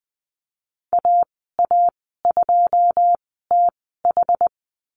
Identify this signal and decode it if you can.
Morse code, heard as a keyed tone, about 20 wpm, spaced as for 15 wpm, 700 Hz. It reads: AA2TH
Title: 167 Lesson 27g - Number 7 - Call Signs 20wpm